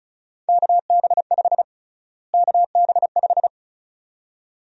Morse code, heard as a keyed tone, about 36 wpm, 700 Hz. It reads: KB5 KB5
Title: USA Call Sign Prefixes - Repeated 2x - 35wpm